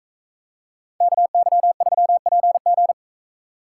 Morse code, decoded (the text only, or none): KY3PG